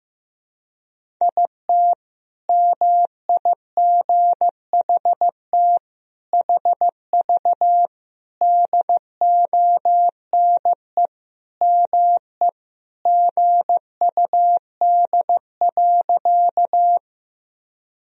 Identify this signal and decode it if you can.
Morse code, heard as a keyed tone, about 15 wpm, 700 Hz. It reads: IT MIGHT HV DONE ME GUD.
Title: Sentences from Top 500 Words and Abbreviations 15wpm